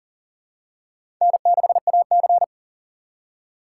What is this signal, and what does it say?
Morse code, heard as a keyed tone, about 40 wpm, 700 Hz. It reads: N6AC
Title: USA Call Signs - 4 Characters - 3x Character Spacing - 40wpm